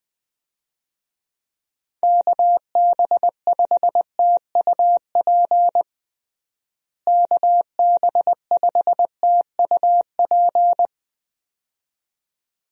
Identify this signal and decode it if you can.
Morse code, heard as a keyed tone, about 21 wpm, 700 Hz. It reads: KB5TUP KB5TUP
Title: USA Call Signs - 6 Characters - Repeated 2x - 20wpm